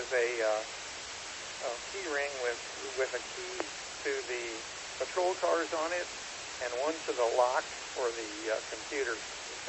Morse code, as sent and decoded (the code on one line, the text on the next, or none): none